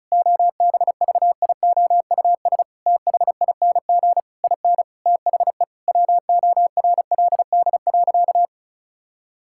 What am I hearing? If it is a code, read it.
Morse code, heard as a keyed tone, about 36 wpm, 700 Hz. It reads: OBVIOUS THING IN THE WORLD.
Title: Sentences from Top 1600 Words 35wpm